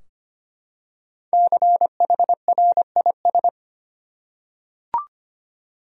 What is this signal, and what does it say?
Morse code, heard as a keyed tone, about 25 wpm, 700 Hz. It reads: CHRIS
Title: USA Common Names - Max 5 Letters - 25wpm